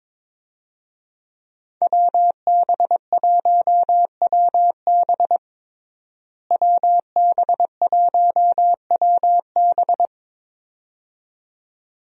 Morse code, heard as a keyed tone, about 22 wpm, 700 Hz. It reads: WB1WB WB1WB
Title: USA Call Signs - 5 Characters - Repeated 2x - 22wpm